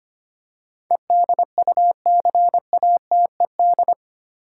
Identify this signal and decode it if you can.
Morse code, heard as a keyed tone, about 25 wpm, 690 Hz. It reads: EDUCATED
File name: 8 Letter Words - First 3 Letters Given - 25wpm